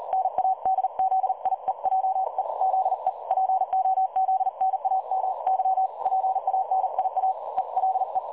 Morse code, heard as a keyed tone, about 39 wpm, 760 Hz. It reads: NADZIEJE ZE POZNAMY PANIA ELE
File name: dlazaawansowanych / nagraniazpasma / websdr recording 3565.0kHz